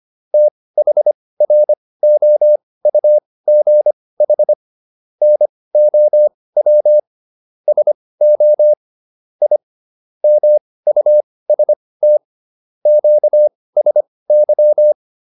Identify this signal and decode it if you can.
Morse code, heard as a keyed tone, about 25 wpm, spaced as for 18 wpm, 600 Hz. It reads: THROUGH NOW SO I MUST QSY
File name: QSO410 18